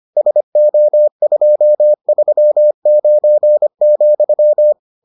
Morse code, heard as a keyed tone, about 25 wpm, 600 Hz. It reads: SO239,